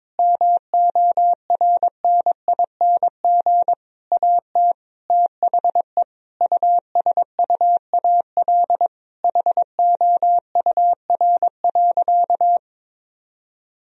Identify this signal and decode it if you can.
Morse code, heard as a keyed tone, about 23 wpm, 700 Hz. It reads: MORNING AT THE USUAL HOUR.
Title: Sentences from Top 1000 Words - Review Part 2 - 22wpm